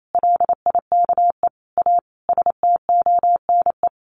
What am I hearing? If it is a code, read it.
Morse code, heard as a keyed tone, about 28 wpm, 700 Hz. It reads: LIKE A STONE
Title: A Sound Of Thunder 28wpm